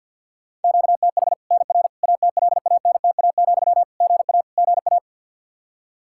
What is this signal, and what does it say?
Morse code, heard as a keyed tone, about 52 wpm, 700 Hz. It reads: QTH NR ATLANTA, GA GA
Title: K1USN SST QSO Practice 50wpm